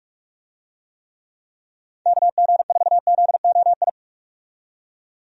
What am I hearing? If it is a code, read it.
Morse code, heard as a keyed tone, about 47 wpm, 700 Hz. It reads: KG4ZOI
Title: USA Call Signs - 6 Characters - 45wpm